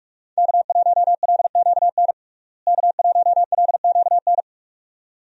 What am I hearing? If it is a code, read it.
Morse code, heard as a keyed tone, about 46 wpm, 700 Hz. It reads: K1LQN K1LQN
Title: USA Call Signs - 5 Characters - Repeated 2x - 45wpm